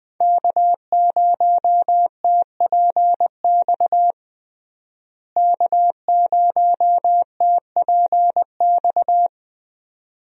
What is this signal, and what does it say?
Morse code, heard as a keyed tone, about 21 wpm, 700 Hz. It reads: K0TPX K0TPX
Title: USA Call Signs - 5 Characters - Repeated 2x - 20wpm